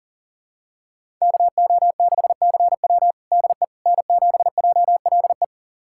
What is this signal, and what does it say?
Morse code, heard as a keyed tone, about 37 wpm, 700 Hz. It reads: KO6CW DE N7JLE
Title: USA Callsign DE Callsign - 5 Characters - 40wpm